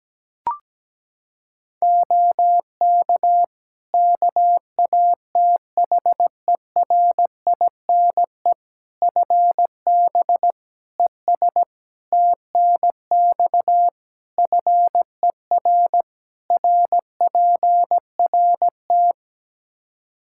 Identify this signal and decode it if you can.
Morse code, heard as a keyed tone, about 17 wpm, 700 Hz. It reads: OK KATHERINE FB ES TNX FER RPRT
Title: K1USN SST QSO Practice 17wpm